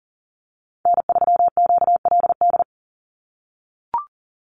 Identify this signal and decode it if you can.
Morse code, heard as a keyed tone, about 40 wpm, 700 Hz. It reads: N3QLD
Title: USA Call Signs 40wpm